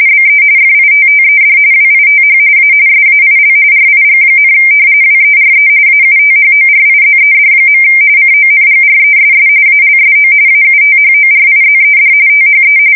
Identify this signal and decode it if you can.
RTTY, about 45.45 baud. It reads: BE FITTED WITH A SAIL RIG, WERE AUTHORIZED. CONGRESS ALSO AUTHORIZED THE
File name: fsk170 / usnavy3